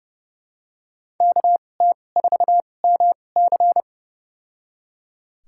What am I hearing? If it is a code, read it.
Morse code, heard as a keyed tone, about 29 wpm, spaced as for 21 wpm, 700 Hz. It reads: KT4MC
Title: USA Call Signs - 5 Characters - 2x Character Spacing - 30wpm